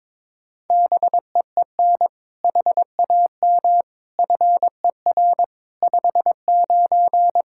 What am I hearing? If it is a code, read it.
Morse code, heard as a keyed tone, about 22 wpm, 700 Hz. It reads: BEEN HAM FER 59
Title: Practice QSO - Exchange 3 - More Info - 22wpm